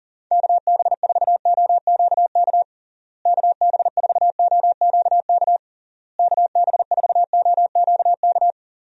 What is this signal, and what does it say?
Morse code, heard as a keyed tone, about 37 wpm, 700 Hz. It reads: KB4OQK KB4OQK KB4OQK
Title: USA Call Signs - 6 Characters - Repeated 3x - 40wpm